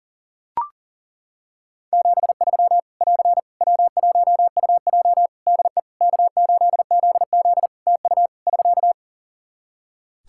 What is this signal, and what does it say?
Morse code, heard as a keyed tone, about 40 wpm, 700 Hz. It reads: 73 <AR> W1UJ DE K8ZZ TU <SK>
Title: Practice QSO - Exchange 4 - Ending - 40wpm